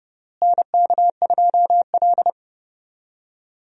Morse code, heard as a keyed tone, about 30 wpm, 700 Hz. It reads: NK2L
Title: USA Call Signs - 4 Characters - Repeated 3x - 30wpm